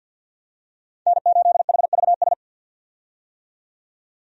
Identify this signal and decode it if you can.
Morse code, heard as a keyed tone, about 52 wpm, 700 Hz. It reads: N8HVS
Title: USA Call Signs - 5 Characters - 50wpm